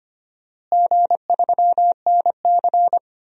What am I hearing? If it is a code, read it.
Morse code, heard as a keyed tone, about 25 wpm, 700 Hz. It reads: G3NC
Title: DXCC-100 Call Signs - 4 Characters - Repeated 3x - 25wpm